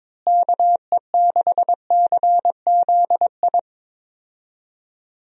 Morse code, heard as a keyed tone, about 21 wpm, 700 Hz. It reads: KE6CZI
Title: USA Call Signs - 6 Characters - Repeated 2x - 22wpm